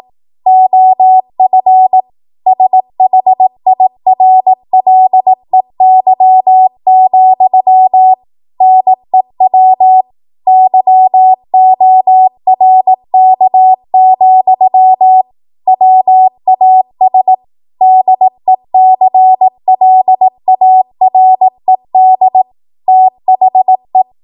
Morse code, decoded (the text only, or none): OF SHIRLEY, NEW YORK, WAS DECLARED THE